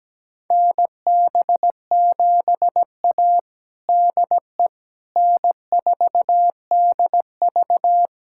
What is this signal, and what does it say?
Morse code, heard as a keyed tone, about 17 wpm, 700 Hz. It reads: NB7A DE N4DV
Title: USA Callsign DE Callsign - 4 Characters - 17wpm